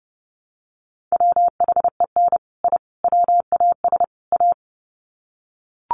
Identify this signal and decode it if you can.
Morse code, heard as a keyed tone, about 30 wpm, 700 Hz. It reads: WHEN I WAS A
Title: Sets of 4 Words 30wpm